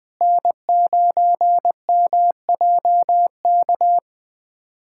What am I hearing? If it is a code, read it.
Morse code, heard as a keyed tone, about 20 wpm, 700 Hz. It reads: N9MJK
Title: USA Callsign DE Callsign - 5 Characters - 20wpm